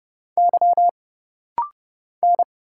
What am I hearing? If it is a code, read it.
Morse code, heard as a keyed tone, about 29 wpm, 700 Hz. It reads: Y N
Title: Single Letter-Number Alt - Rapid-Fire 30wpm